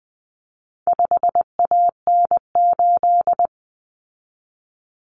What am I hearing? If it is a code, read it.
Morse code, heard as a keyed tone, about 20 wpm, 700 Hz. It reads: 5AN8
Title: Common Cut Numbers - 4-Digit 20wpm